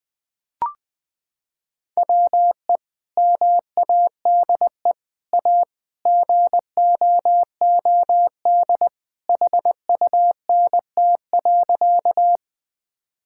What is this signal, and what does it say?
Morse code, heard as a keyed tone, about 21 wpm, 700 Hz. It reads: WE MADE A GOOD HUNT.
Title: Sentences from Top 1600 Words 20wpm